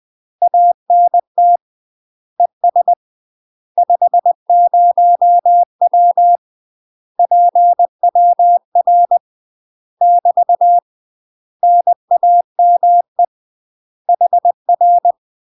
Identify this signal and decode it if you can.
Morse code, heard as a keyed tone, about 20 wpm, 700 Hz. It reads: ANT ES 50W PWR <BT> NAME HR
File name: UK MASTER QSOs TRAINING / 20WPM / UK MASTER QSOs 0023